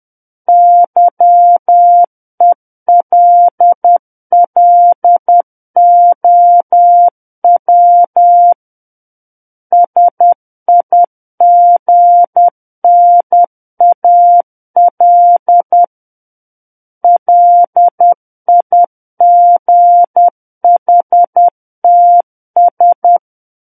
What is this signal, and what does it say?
Morse code, heard as a keyed tone, about 10 wpm, 700 Hz. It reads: YELLOW SIGNAL LIGHTS
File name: Book One / 01-Chapter 1 - 10 WPM